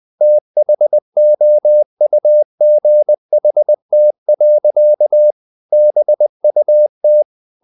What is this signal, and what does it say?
Morse code, heard as a keyed tone, about 20 wpm, 600 Hz. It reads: THOUGHT. BUT
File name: the old man and the sea part 2 20wpm